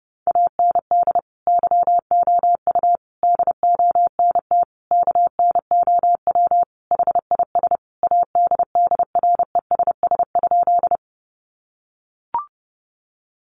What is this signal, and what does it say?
Morse code, heard as a keyed tone, about 30 wpm, 700 Hz. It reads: AND YOU DONT KNOW HIS ADDRESS?
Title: Sentences from Top 1000 Words 30wpm